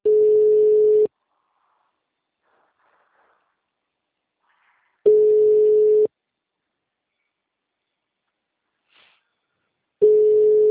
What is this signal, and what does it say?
Ringback tone (Europe, 425 Hz).